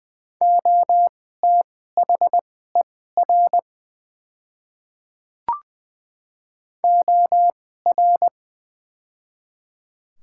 Morse code, heard as a keyed tone, about 20 wpm, spaced as for 15 wpm, 700 Hz. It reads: OTHER OR
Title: 2x Character Spacing - Top 100 Words - 20wpm